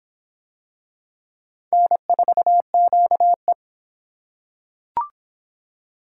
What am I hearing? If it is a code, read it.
Morse code, heard as a keyed tone, about 25 wpm, 700 Hz. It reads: N4QE